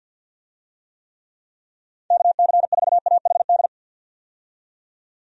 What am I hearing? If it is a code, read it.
Morse code, heard as a keyed tone, about 51 wpm, 700 Hz. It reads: KC4AHD
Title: USA Call Signs - 6 Characters - 50wpm